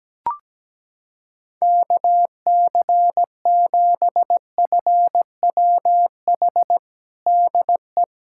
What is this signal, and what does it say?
Morse code, heard as a keyed tone, about 17 wpm, 700 Hz. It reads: KC7FWH DE
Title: USA Callsign DE Callsign - 6 Characters - 17wpm